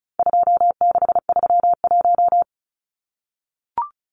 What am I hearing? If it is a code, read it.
Morse code, heard as a keyed tone, about 34 wpm, 700 Hz. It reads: 2631
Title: Numbers - 4-Digit 35wpm